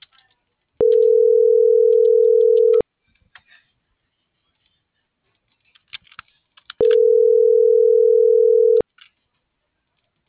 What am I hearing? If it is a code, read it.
Ringback tone (North America), 440 + 480 Hz.